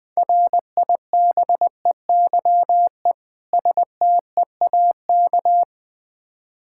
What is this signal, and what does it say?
Morse code, heard as a keyed tone, about 20 wpm, 700 Hz. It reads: RIBEYE STEAK
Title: Difficult Words with Hints - 20wpm